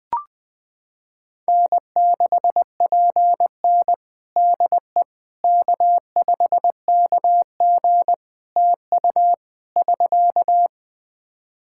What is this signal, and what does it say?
Morse code, heard as a keyed tone, about 21 wpm, 700 Hz. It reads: N6PN DE K5KG TU <SK>
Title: Practice QSO - Exchange 4 - Ending - 20wpm